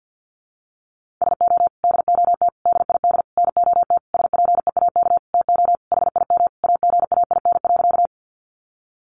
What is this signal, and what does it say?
Morse code, heard as a keyed tone, about 50 wpm, 700 Hz. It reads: SO DOT DID NOT SPEAK TO HIM AGAIN.
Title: Sentences from Top 1500 Words 50wpm